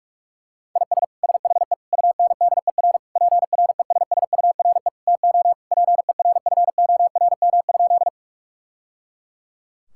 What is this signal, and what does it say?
Morse code, heard as a keyed tone, about 46 wpm, 700 Hz. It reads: IS SHE UNDER PRESSURE TO PERFORM?
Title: Sentences from Top 1000 Words - Review Part 6 - 45wpm